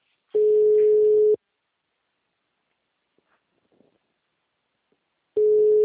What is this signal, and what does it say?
Ringback tone (Europe, 425 Hz).